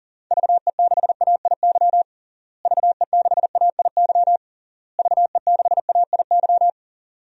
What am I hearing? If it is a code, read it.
Morse code, heard as a keyed tone, about 40 wpm, 700 Hz. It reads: VE6AIY VE6AIY VE6AIY
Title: DXCC-100 Call Signs - 6 Characters - Repeated 3x - 40wpm